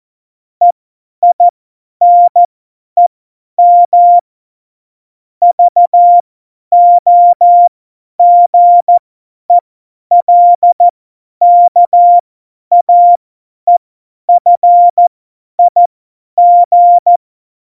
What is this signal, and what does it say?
Morse code, heard as a keyed tone, about 14 wpm, spaced as for 10 wpm, 700 Hz. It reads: EINEM VOGELKAEFIG
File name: Morsekurs / Witze / Witz 04. September 2018, 50 F70